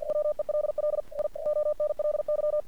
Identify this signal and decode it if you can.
Morse code, heard as a keyed tone, about 47 wpm, 600 Hz. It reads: WELL I ONLY
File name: archived / CRSnet